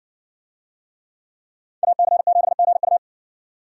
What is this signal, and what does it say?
Morse code, heard as a keyed tone, about 60 wpm, 700 Hz. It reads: AC7GU